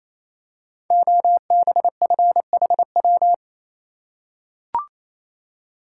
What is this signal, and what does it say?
Morse code, heard as a keyed tone, about 27 wpm, 700 Hz. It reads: OBFHW
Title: Code Groups - 5 Letters - 28wpm